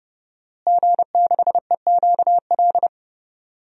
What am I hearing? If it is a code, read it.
Morse code, heard as a keyed tone, about 30 wpm, 700 Hz. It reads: G6EQL
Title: DXCC-100 Call Signs - 5 Characters - Repeated 3x - 30wpm